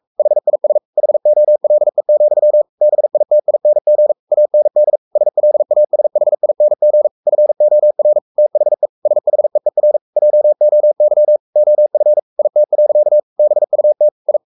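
Morse code, heard as a keyed tone, about 44 wpm, 600 Hz. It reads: HIS HOLE, BITING AND SLASHING FOR THE SHEER JOY OF IT. BUT I
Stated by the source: Morse code at 43 wpm sending Ken Follett's novel Never in English, from CW med Gnister.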